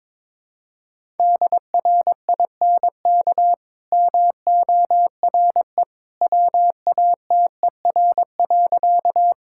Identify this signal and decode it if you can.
Morse code, heard as a keyed tone, about 21 wpm, 690 Hz. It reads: DRINK MORE WATER.
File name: practice / 22wpm / Easy Sentences with Hints - 22wpm